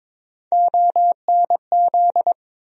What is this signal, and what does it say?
Morse code, heard as a keyed tone, about 22 wpm, 700 Hz. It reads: ONZ
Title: USA Call Sign Postfixes - 3 Letters 22wpm